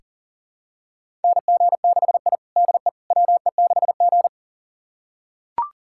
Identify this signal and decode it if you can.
Morse code, heard as a keyed tone, about 37 wpm, 700 Hz. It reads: NG6I DE WE6G
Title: USA Callsign DE Callsign - 4 Characters - 40wpm